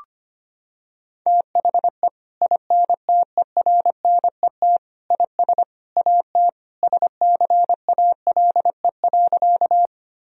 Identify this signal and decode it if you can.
Morse code, heard as a keyed tone, about 25 wpm, 700 Hz. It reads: THE INTERNET IS AT SCALE.
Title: Sentences from Top 1000 Words - Review Part 4 - 25wpm